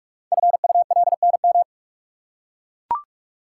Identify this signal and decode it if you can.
Morse code, heard as a keyed tone, about 44 wpm, 700 Hz. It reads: FULNM
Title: Code Groups - 5 Letters - 45wpm